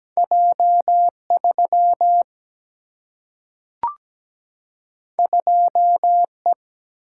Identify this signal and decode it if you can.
Morse code, heard as a keyed tone, about 17 wpm, 700 Hz. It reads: J3 2E
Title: Single-Letter Single-Number 17wpm